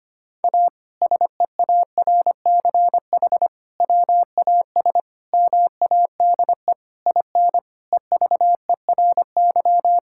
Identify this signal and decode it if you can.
Morse code, heard as a keyed tone, about 25 wpm, 700 Hz. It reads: A SEARCH WAS MADE IN EVERY
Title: Sentences from Top 1000 Words - Review Part 5 - 25wpm